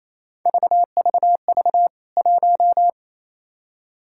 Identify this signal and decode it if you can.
Morse code, heard as a keyed tone, about 28 wpm, 700 Hz. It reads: VVV 1
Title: K1USN SST QSO Practice 28wpm